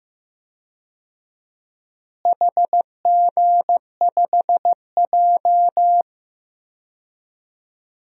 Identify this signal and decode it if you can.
Morse code, heard as a keyed tone, about 15 wpm, 700 Hz. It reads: HG5J
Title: DXCC-100 Call Signs - 4 Characters - Repeated 3x - 15wpm